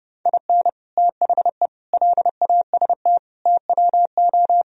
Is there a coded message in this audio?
Yes: Morse code from 0.3 to 4.6 s, heard as a keyed tone, about 30 wpm, 700 Hz. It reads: IN THE LAST TWO